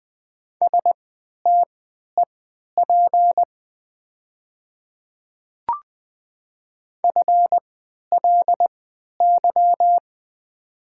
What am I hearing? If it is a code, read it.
Morse code, heard as a keyed tone, about 20 wpm, spaced as for 11 wpm, 700 Hz. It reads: STEP FLY
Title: 3x Character Spacing - Top 500 Words - 20wpm